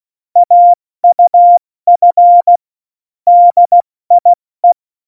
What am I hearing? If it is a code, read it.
Morse code, heard as a keyed tone, about 16 wpm, 700 Hz. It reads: AUF DIE
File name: Morsekurs / Witze / Witz 18 August 2020, 70 F80